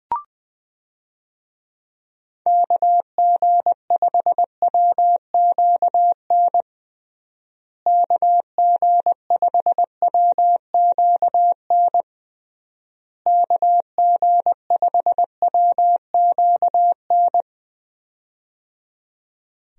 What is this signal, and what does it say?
Morse code, heard as a keyed tone, about 20 wpm, 700 Hz. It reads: KG5WQN KG5WQN KG5WQN
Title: USA Call Signs - 6 Characters - Repeated 3x - 20wpm